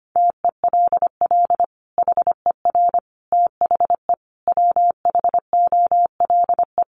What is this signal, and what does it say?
Morse code, heard as a keyed tone, about 25 wpm, 700 Hz. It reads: TELL HER THE WHOLE
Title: Sentences from Top 800 Words 25wpm